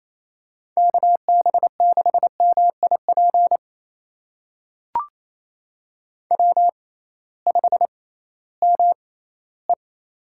Morse code, heard as a keyed tone, about 28 wpm, spaced as for 16 wpm, 700 Hz. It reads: KB6MIP W5ME
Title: USA Call Signs - 6x Character Spacing - 28wpm